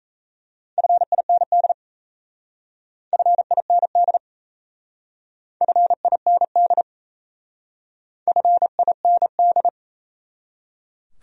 Morse code, heard as a keyed tone, about 35 wpm, 700 Hz. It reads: FIND FIND FIND FIND
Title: Speed-Racing Top 100 Words 28wpm